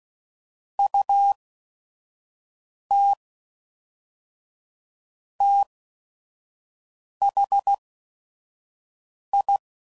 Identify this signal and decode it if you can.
Morse code, heard as a keyed tone, about 16 wpm, spaced as for 5 wpm, 800 Hz. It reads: UT THI
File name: CW Code Groups @5WPM / QSOs / QSO WD0FBT